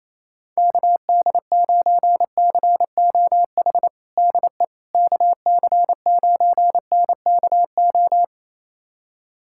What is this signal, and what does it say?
Morse code, heard as a keyed tone, about 28 wpm, 700 Hz. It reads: KD9COH DE KC9NKO
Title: USA Callsign DE Callsign - 6 Characters - 28wpm